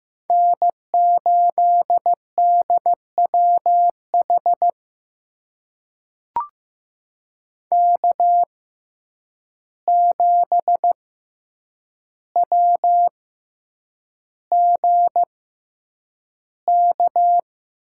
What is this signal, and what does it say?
Morse code, heard as a keyed tone, about 15 wpm, spaced as for 8 wpm, 700 Hz. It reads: N8DWH K7WGK
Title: USA Call Signs - 5 Characters - 6x Character Spacing - 15wpm